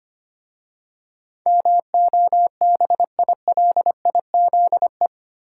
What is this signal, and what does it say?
Morse code, heard as a keyed tone, about 25 wpm, 700 Hz. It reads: MOBILIZE
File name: 8 Letter Words - First 3 Letters Given - 25wpm